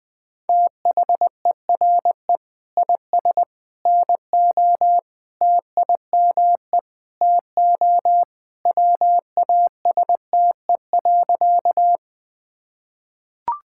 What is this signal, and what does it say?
Morse code, heard as a keyed tone, about 21 wpm, 700 Hz. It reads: THERE IS NO TIME TO WASTE.
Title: Sentences with 3-Word Phrases - 20wpm